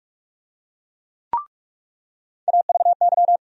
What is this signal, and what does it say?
Morse code, heard as a keyed tone, about 42 wpm, 700 Hz. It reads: AVY
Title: USA Call Sign Postfixes - 3 Letters 45wpm